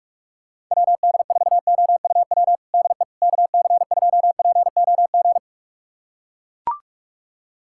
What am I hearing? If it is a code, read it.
Morse code, heard as a keyed tone, about 42 wpm, 700 Hz. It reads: WD4OUW DE KC2POG
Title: USA Callsign DE Callsign 45wpm